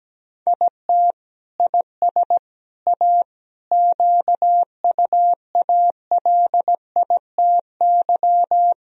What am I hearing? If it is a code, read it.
Morse code, heard as a keyed tone, about 17 wpm, 700 Hz. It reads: IT IS A QUALITY
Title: Sentences from Top 1000 Words - Review Part 4 - 17wpm